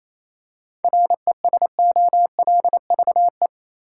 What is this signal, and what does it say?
Morse code, heard as a keyed tone, about 28 wpm, 700 Hz. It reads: RESOLVE
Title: Seven Letter Words - 28wpm